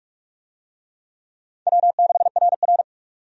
Morse code, heard as a keyed tone, about 46 wpm, 700 Hz. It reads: W6RR
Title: USA Call Signs - 4 Characters - 45wpm